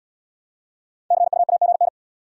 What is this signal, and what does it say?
Morse code, heard as a keyed tone, about 69 wpm, 700 Hz. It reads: CUAGN